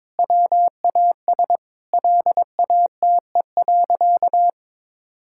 Morse code, heard as a keyed tone, about 22 wpm, 700 Hz. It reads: WAS LATE.
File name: practice / 22wpm / Sentences with 2-Word Phrases - 22wpm